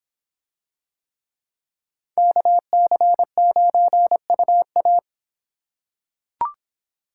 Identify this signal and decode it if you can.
Morse code, heard as a keyed tone, about 26 wpm, 700 Hz. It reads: KC9UA